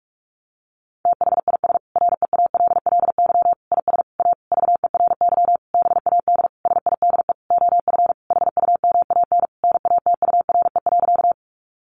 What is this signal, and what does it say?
Morse code, heard as a keyed tone, about 44 wpm, 700 Hz. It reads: THIS REALLY IS A VERY BAD SIDE OF HUMAN NATURE.
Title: Sentences from Top 700 Words 45wpm